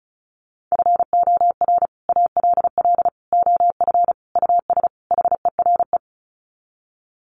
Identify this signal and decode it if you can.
Morse code, heard as a keyed tone, about 34 wpm, 700 Hz. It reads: FOR ALL OF US HERE
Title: Sets of 5 Words 35wpm